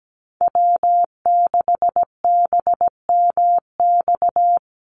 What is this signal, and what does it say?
Morse code, heard as a keyed tone, about 17 wpm, 700 Hz. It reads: W6BMX
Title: USA Call Signs Repeated 3x 17wpm